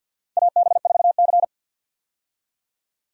Morse code, heard as a keyed tone, about 50 wpm, 700 Hz. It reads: AB4C
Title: USA Call Signs - 4 Characters - 50wpm